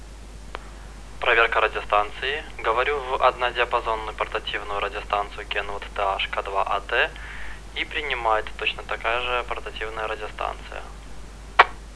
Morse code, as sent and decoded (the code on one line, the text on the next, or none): none